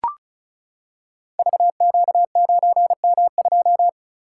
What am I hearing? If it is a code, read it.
Morse code, heard as a keyed tone, about 36 wpm, 700 Hz. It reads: VQ9M2
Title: Code Groups - 5 Letter:Numbers - 35wpm